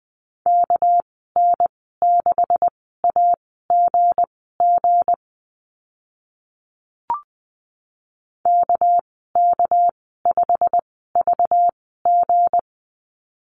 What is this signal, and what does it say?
Morse code, heard as a keyed tone, about 20 wpm, spaced as for 14 wpm, 700 Hz. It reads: KN6AGG KK5VG
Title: 213 Lesson 34g - Number 8 - Call Signs 20wpm